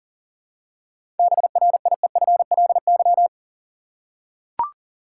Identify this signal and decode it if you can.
Morse code, heard as a keyed tone, about 38 wpm, 700 Hz. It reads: BRIEFLY
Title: Seven Letter Words - 40wpm